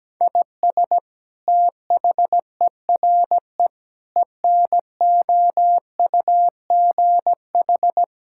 Morse code, heard as a keyed tone, about 17 wpm, 700 Hz. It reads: IS THERE ENOUGH
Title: Sentences from Top 1000 Words - Review Part 5 - 17wpm